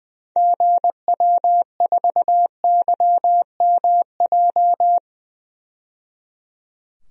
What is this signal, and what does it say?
Morse code, heard as a keyed tone, about 20 wpm, 700 Hz. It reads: GW4YMJ
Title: DXCC-100 Call Signs - 6 Characters - 20wpm